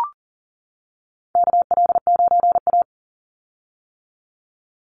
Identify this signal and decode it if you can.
Morse code, heard as a keyed tone, about 40 wpm, 700 Hz. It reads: KL9A
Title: Call Signs - 40wpm